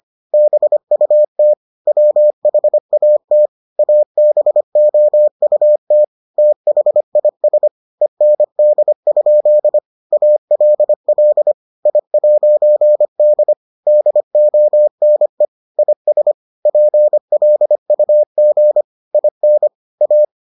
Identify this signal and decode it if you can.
Morse code, heard as a keyed tone, about 25 wpm, 600 Hz. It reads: BUT WHAT ABOUT THIS END? ALL I'D DONE IS PLUG IN A